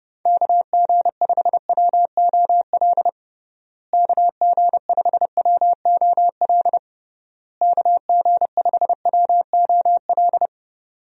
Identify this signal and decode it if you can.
Morse code, heard as a keyed tone, about 30 wpm, 700 Hz. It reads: KG5WOL KG5WOL KG5WOL
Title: USA Call Signs - 6 Characters - Repeated 3x - 30wpm